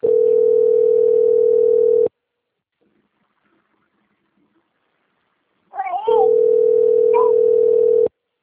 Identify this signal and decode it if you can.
Ringback tone (North America), 440 + 480 Hz.
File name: out- - - - - .9763